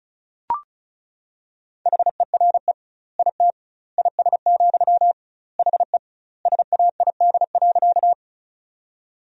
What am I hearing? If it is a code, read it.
Morse code, heard as a keyed tone, about 36 wpm, 700 Hz. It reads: HERE IT IS, HE SAID.
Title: 2x Word-Spacing - Sentences from Top 200 Words - 35wpm